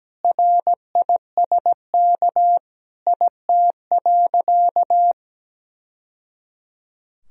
Morse code, heard as a keyed tone, about 17 wpm, 700 Hz. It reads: RISK IT.
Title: Sentences from Top 1000 Words - Review Part 4 - 17wpm